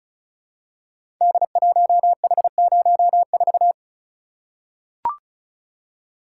Morse code, heard as a keyed tone, about 36 wpm, 700 Hz. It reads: D1H04
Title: Code Groups - 5 Letter:Numbers - 35wpm